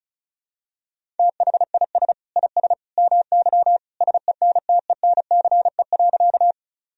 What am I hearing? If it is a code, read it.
Morse code, heard as a keyed tone, about 36 wpm, 690 Hz. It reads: THIS IS MY SENTENCE.
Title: Sentences from Top 1000 Words - Review Part 1 - 35wpm